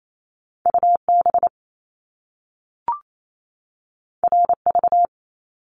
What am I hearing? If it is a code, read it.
Morse code, heard as a keyed tone, about 28 wpm, 700 Hz. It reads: UB RV
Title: Top Two Letter Combinations in Words 28wpm